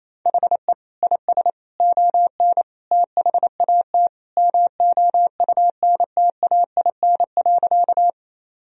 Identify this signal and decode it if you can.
Morse code, heard as a keyed tone, about 28 wpm, 700 Hz. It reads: HE IS ON THAT MOUNTAIN.
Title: Sentences from Top 1000 Words - Review Part 1 - 28wpm